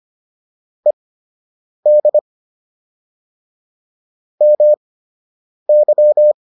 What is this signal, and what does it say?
Morse code, heard as a keyed tone, about 25 wpm, spaced as for 8 wpm, 600 Hz. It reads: ED MY